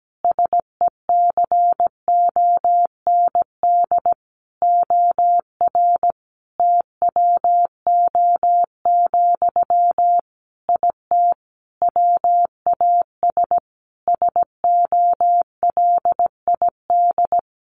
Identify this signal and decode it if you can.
Morse code, heard as a keyed tone, about 17 wpm, 700 Hz. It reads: SECOND OR TWO, IT WAS SOLID
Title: Sentences from Top 1500 Words 17wpm